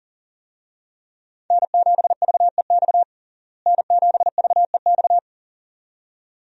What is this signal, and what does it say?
Morse code, heard as a keyed tone, about 40 wpm, 700 Hz. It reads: N7VEX N7VEX
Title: USA Call Signs - 5 Characters - Repeated 2x - 40wpm